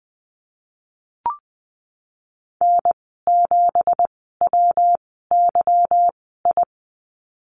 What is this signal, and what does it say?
Morse code, heard as a keyed tone, about 20 wpm, spaced as for 15 wpm, 700 Hz. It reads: N7WYI
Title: 167 Lesson 27g - Number 7 - Call Signs 20wpm